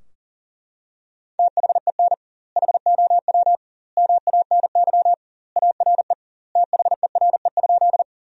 Morse code, heard as a keyed tone, about 37 wpm, 700 Hz. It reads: THEN HOW MANY ARE THERE?
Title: 2x Word-Spacing - Sentences from Top 100 Words - 40wpm